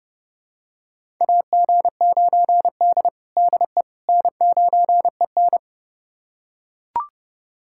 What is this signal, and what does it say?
Morse code, heard as a keyed tone, about 30 wpm, 700 Hz. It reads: AG9D DE N9EN
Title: USA Callsign DE Callsign - 4 Characters - 30wpm